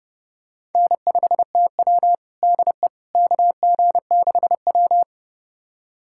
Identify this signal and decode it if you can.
Morse code, heard as a keyed tone, about 30 wpm, 700 Hz. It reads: N5TW DE KG6W
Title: USA Callsign DE Callsign - 4 Characters - 30wpm